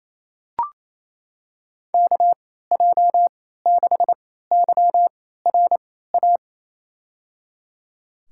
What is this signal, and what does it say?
Morse code, heard as a keyed tone, about 27 wpm, spaced as for 16 wpm, 700 Hz. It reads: KJ6YRA
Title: USA Call Signs - 6 Characters - 3x Character Spacing - 28wpm